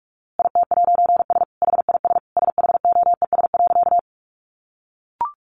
Morse code, heard as a keyed tone, about 44 wpm, 700 Hz. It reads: IT'S HIS SHOES.
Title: A Sound Of Thunder 45wpm